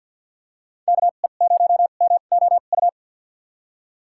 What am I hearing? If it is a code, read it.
Morse code, heard as a keyed tone, about 47 wpm, spaced as for 35 wpm, 700 Hz. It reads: KE0MOU
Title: USA Call Signs - 2x Character Spacing - 50wpm